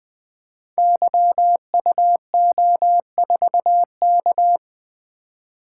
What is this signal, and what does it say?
Morse code, heard as a keyed tone, about 21 wpm, 700 Hz. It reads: YUO4K
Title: Code Groups - 5 Letter:Numbers - 20wpm